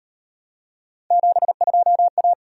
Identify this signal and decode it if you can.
Morse code, heard as a keyed tone, about 38 wpm, 700 Hz. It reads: 72A